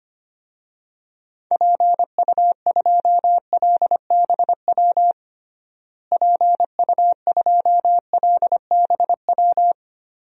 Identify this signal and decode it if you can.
Morse code, heard as a keyed tone, about 25 wpm, 700 Hz. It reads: PU2LBW PU2LBW
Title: DXCC-100 Call Signs - 6 Characters - Repeated 3x - 25wpm